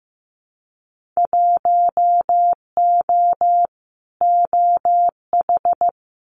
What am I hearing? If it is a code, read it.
Morse code, heard as a keyed tone, about 15 wpm, 700 Hz. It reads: 1O OH
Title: Winter Field Day Exchanges 15wpm